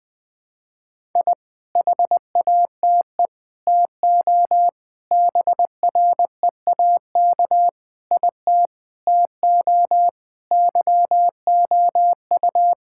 Morse code, heard as a keyed tone, about 20 wpm, 700 Hz. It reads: I HATE TO BREAK IT TO YOU
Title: Sets of 8 Words - 20wpm